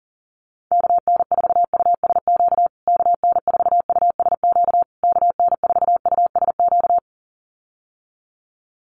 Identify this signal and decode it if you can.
Morse code, heard as a keyed tone, about 40 wpm, 700 Hz. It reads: KN4USQ KN4USQ KN4USQ
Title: USA Call Signs Repeated 3x 40wpm